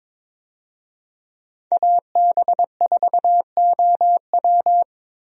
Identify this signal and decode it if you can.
Morse code, heard as a keyed tone, about 22 wpm, 700 Hz. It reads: AB4OW
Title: USA Call Signs - 5 Characters - Repeated 3x - 22wpm